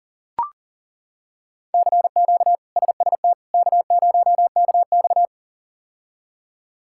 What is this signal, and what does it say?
Morse code, heard as a keyed tone, about 40 wpm, 700 Hz. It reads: CQ SST K0KX
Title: K1USN SST QSO Practice - 40wpm